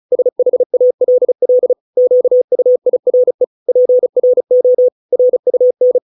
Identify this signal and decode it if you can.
Morse code, heard as a keyed tone, about 34 wpm, 500 Hz. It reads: SHALL QUIRE PRO RUN